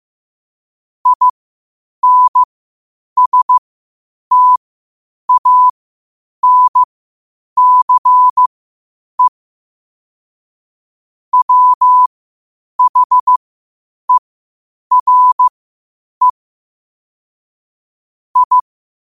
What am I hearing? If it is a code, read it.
Morse code, heard as a keyed tone, about 15 wpm, spaced as for 8 wpm, 1000 Hz. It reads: INSTANCE WHERE I
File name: qotdfiles / 8wpm